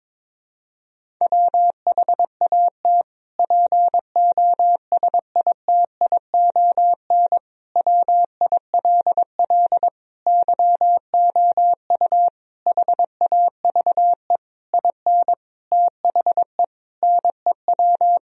Morse code, heard as a keyed tone, about 22 wpm, 700 Hz. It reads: WHAT POSITION WILL YOU HAVE IN THE NEW
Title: Sentences from Top 1000 Words - Review Part 3 - 22wpm